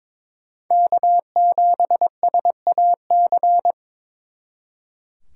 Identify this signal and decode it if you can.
Morse code, heard as a keyed tone, about 26 wpm, 700 Hz. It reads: K7SAC
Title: Speed Racing - USA 1x3 Call Signs 22wpm